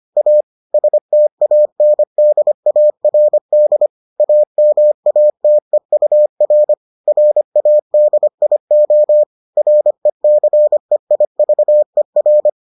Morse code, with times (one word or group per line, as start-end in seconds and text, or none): 0.2-0.4 A
0.7-3.9 STANDARD
4.2-6.7 AMATEUR
7.1-9.2 RADIO
9.6-12.5 RECEIVER